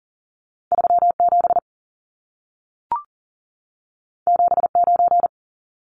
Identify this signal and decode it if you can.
Morse code, heard as a keyed tone, about 40 wpm, 700 Hz. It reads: 37 79
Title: Numbers - 2-Digit 40wpm